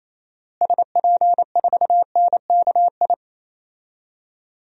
Morse code, heard as a keyed tone, about 28 wpm, 700 Hz. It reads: SP4NKI
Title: DXCC-100 Call Signs - 6 Characters - Repeated 2x - 28wpm